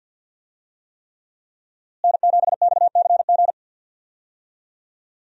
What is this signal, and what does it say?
Morse code, heard as a keyed tone, about 51 wpm, 700 Hz. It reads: N7XCG
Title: USA Call Signs - 5 Characters - Repeated 3x - 50wpm